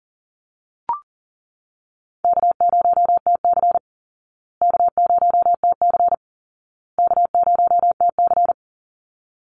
Morse code, heard as a keyed tone, about 40 wpm, 700 Hz. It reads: K0TC K0TC K0TC
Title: Call Signs Repeated 2x - 40wpm